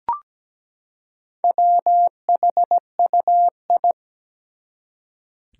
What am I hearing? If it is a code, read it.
Morse code, heard as a keyed tone, about 17 wpm, 700 Hz. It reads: WHUI
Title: Four Challenging Letter-Number Combinations 17wpm